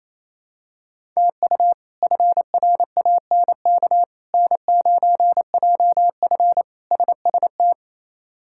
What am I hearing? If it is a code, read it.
Morse code, heard as a keyed tone, about 28 wpm, 700 Hz. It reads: TU FRANK N9JF SST
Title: K1USN SST QSO Practice - 28wpm